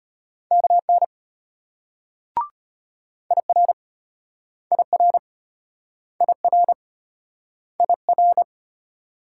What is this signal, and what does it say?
Morse code, heard as a keyed tone, about 34 wpm, 700 Hz. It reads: KN IR IR IR IR